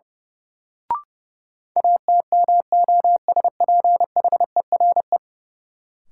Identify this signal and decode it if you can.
Morse code, heard as a keyed tone, about 30 wpm, 700 Hz. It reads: ATMOSPHERE
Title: Sentences from Top 1600 Words - New Word Drill - 30wpm